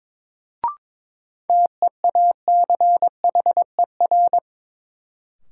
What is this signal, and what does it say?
Morse code, heard as a keyed tone, about 22 wpm, 700 Hz. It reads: TEACHER
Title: Sentences from Top 900 Words - New Word Drill - 22wpm